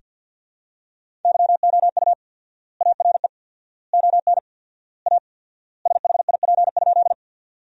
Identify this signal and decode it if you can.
Morse code, heard as a keyed tone, about 48 wpm, 700 Hz. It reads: YOU ARE ON A SHIP?
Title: 4x Word-Spacing - Sentences from Top 400 Words - 50wpm